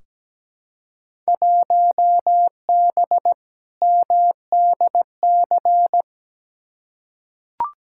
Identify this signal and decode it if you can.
Morse code, heard as a keyed tone, about 17 wpm, 700 Hz. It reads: 1B MDC
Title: Field Day Exchanges - 17wpm